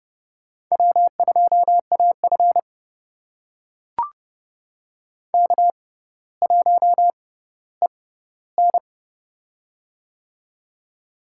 Morse code, heard as a keyed tone, about 32 wpm, spaced as for 16 wpm, 700 Hz. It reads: W2AF K1EN
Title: USA Call Signs - 4 Characters - 6x Character Spacing - 30wpm